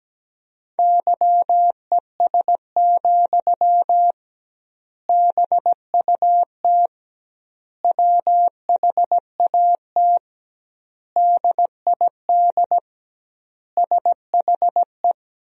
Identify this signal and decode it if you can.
Morse code, heard as a keyed tone, about 17 wpm, 700 Hz. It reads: YES, BUT WHAT DID SHE
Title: 2x Word-Spacing - Sentences from Top 500 Words - 17wpm